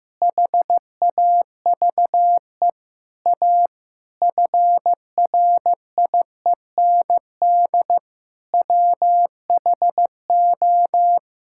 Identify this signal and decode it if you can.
Morse code, heard as a keyed tone, about 15 wpm, 700 Hz. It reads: HAVE A FRIEND WHO
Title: Sentences from Top 1000 Words - Review Part 5 - 15wpm